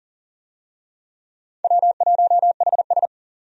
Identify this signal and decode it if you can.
Morse code, heard as a keyed tone, about 40 wpm, 700 Hz. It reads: W1HS
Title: USA Call Signs - 4 Characters - 40wpm